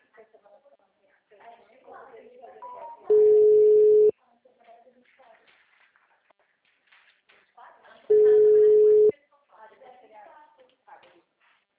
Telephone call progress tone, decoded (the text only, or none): ringback tone